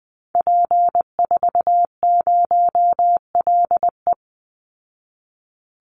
Morse code, heard as a keyed tone, about 20 wpm, 700 Hz. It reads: P40LE
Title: Call Signs Repeated 2x - 20wpm